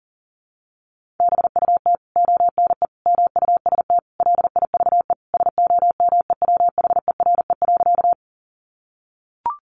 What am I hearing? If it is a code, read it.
Morse code, heard as a keyed tone, about 40 wpm, 700 Hz. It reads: BUT ONE MUST LIVE SOMEWHERE.
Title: Sentences from Top 1400 Words 40wpm